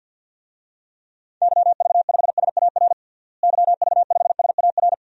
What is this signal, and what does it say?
Morse code, heard as a keyed tone, about 51 wpm, 700 Hz. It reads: YV5SAR YV5SAR
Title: DXCC-100 Call Signs - 6 Characters - Repeated 3x - 50wpm